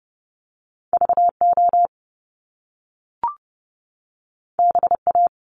Morse code, heard as a keyed tone, about 30 wpm, 700 Hz. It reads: VO BA
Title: Top Two Letter Combinations in Words 30wpm